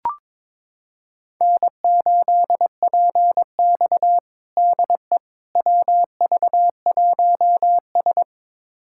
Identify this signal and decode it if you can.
Morse code, heard as a keyed tone, about 22 wpm, 700 Hz. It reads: N8PX DE WV1S
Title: USA Callsign DE Callsign - 4 Characters - 22wpm